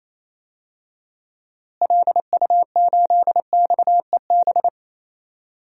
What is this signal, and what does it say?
Morse code, heard as a keyed tone, about 28 wpm, 700 Hz. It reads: LU8XEB
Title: DXCC-100 Call Signs - 6 Characters - Repeated 2x - 28wpm